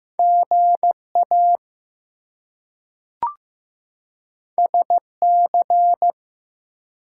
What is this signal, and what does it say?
Morse code, heard as a keyed tone, about 15 wpm, 700 Hz. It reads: GA SC
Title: US States and Canadian Provinces 15wpm